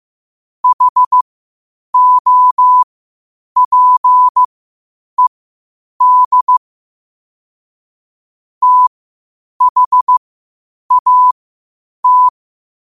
Morse code, heard as a keyed tone, about 15 wpm, spaced as for 8 wpm, 1000 Hz. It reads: HOPED THAT
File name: qotdfiles / 8wpm